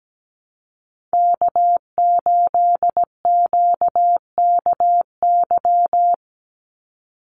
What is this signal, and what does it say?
Morse code, heard as a keyed tone, about 17 wpm, 700 Hz. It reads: K8QKY
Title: Call Signs Repeated 2x - 17wpm